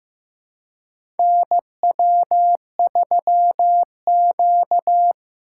Morse code, heard as a keyed tone, about 15 wpm, 700 Hz. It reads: NW3Q
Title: USA Call Signs - 4 Characters - 4x Character Spacing - 15wpm